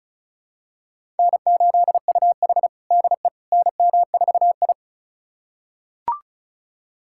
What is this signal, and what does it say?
Morse code, heard as a keyed tone, about 34 wpm, 700 Hz. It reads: N8UH DE NM4I
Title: USA Callsign DE Callsign - 4 Characters - 35wpm